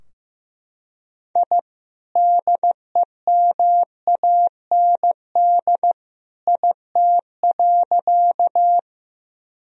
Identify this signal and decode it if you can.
Morse code, heard as a keyed tone, about 15 wpm, 700 Hz. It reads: I DEMAND IT.
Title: Sentences from Top 1000 Words - Review Part 4 - 15wpm